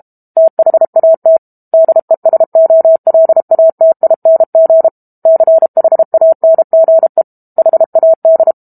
Morse code, heard as a keyed tone, about 32 wpm, 650 Hz. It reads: THAT DESOLATING CHANGE HAD
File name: Book Two / 06-Chapter 6 - 32 WPM